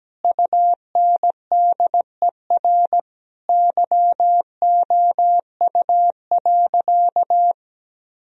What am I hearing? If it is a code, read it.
Morse code, heard as a keyed tone, about 17 wpm, 700 Hz. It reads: UNDER YOU.
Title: Sentences from Top 1000 Words - Review Part 1 - 17wpm